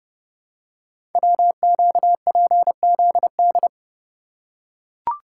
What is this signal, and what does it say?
Morse code, heard as a keyed tone, about 30 wpm, 700 Hz. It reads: WQPZD
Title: Code Groups - 5 Letter:Numbers - 30wpm